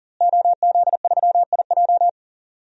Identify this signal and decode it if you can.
Morse code, heard as a keyed tone, about 39 wpm, 700 Hz. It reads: OZ3IJ
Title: DXCC-100 Call Signs - 5 Characters - Repeated 3x - 40wpm